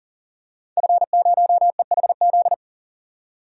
Morse code, heard as a keyed tone, about 40 wpm, 700 Hz. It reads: F0EHZ
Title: DXCC-100 Call Signs - 5 Characters - 40wpm